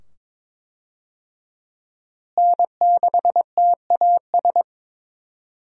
Morse code, heard as a keyed tone, about 21 wpm, 700 Hz. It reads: N6TAS
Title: USA Call Signs - 5 Characters - Repeated 2x - 22wpm